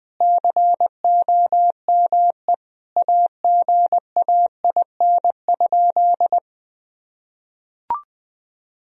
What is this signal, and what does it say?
Morse code, heard as a keyed tone, about 20 wpm, 700 Hz. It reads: COME AGAIN?
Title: Sentences from Top 1000 Words - Review Part 4 - 20wpm